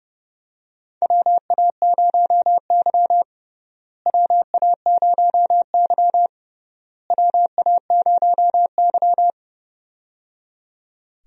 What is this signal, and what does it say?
Morse code, heard as a keyed tone, about 30 wpm, 700 Hz. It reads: WA0Y WA0Y WA0Y
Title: USA Call Signs - 4 Characters - Repeated 3x - 30wpm